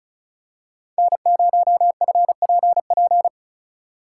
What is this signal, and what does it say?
Morse code, heard as a keyed tone, about 34 wpm, 700 Hz. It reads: N0FPP
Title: USA Call Signs - 3x Character Spacing - 35wpm